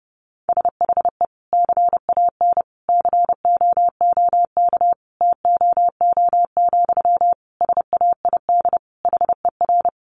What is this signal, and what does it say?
Morse code, heard as a keyed tone, about 30 wpm, 700 Hz. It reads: SHE CAN COOK TOO, SAID HER
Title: Sentences from Top 700 Words 30wpm